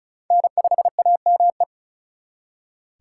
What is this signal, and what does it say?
Morse code, heard as a keyed tone, about 44 wpm, 700 Hz. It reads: N5AME